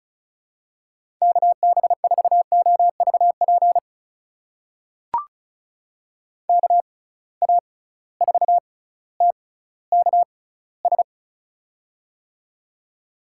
Morse code, heard as a keyed tone, about 36 wpm, spaced as for 18 wpm, 700 Hz. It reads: KB4OVP KA4TKS
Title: USA Call Signs - 6 Characters - 6x Character Spacing - 35wpm